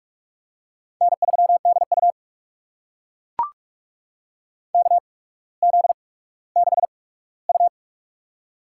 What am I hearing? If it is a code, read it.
Morse code, heard as a keyed tone, about 44 wpm, spaced as for 19 wpm, 700 Hz. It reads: N3DU KZ6U
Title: USA Call Signs - 4 Characters - 8x Character Spacing - 45wpm